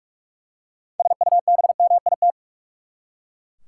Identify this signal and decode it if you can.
Morse code, heard as a keyed tone, about 43 wpm, 700 Hz. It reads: SUBMIT
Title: Six Letter Words - 45wpm